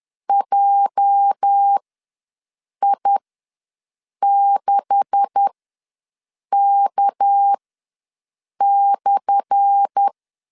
Morse code, heard as a keyed tone, about 10 wpm, spaced as for 6 wpm, 800 Hz. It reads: JI6K/
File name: LEZIONI / 10 SIGNS / 10A SLASH ws 10wpm 4121